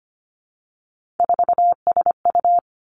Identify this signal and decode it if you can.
Morse code, heard as a keyed tone, about 25 wpm, 700 Hz. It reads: 4SU
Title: Three Challenging Letter-Number Combinations 25wpm